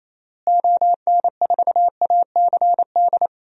Morse code, heard as a keyed tone, about 28 wpm, 700 Hz. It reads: ON4ACD
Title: DXCC-100 Call Signs - 6 Characters - Repeated 2x - 28wpm